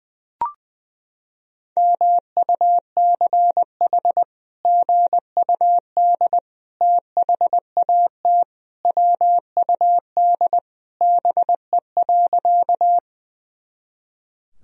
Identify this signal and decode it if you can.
Morse code, heard as a keyed tone, about 19 wpm, 700 Hz. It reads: MUCH GUD THAT WUD BE.
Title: Sentences from Top 200 Words and Abbreviations 20wpm